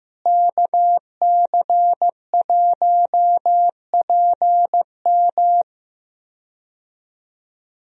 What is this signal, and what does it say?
Morse code, heard as a keyed tone, about 15 wpm, 700 Hz. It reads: KC1PM
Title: USA Call Signs - 5 Characters - Repeated 2x - 15wpm